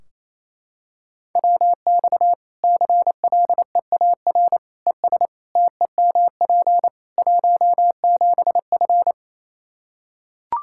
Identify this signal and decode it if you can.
Morse code, heard as a keyed tone, about 28 wpm, 700 Hz. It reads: WX CLEAR ES TEMP 17F
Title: Practice QSO - Exchange 2 - Info - 28wpm